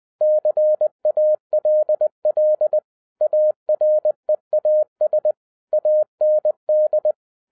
Morse code, heard as a keyed tone, about 20 wpm, 610 Hz. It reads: CALL AREAS AND